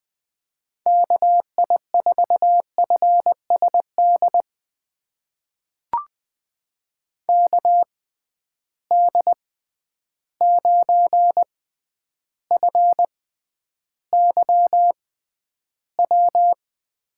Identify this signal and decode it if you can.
Morse code, heard as a keyed tone, about 20 wpm, spaced as for 10 wpm, 700 Hz. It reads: KI4FSD KD9FYW
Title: USA Call Signs - 6 Characters - 6x Character Spacing - 20wpm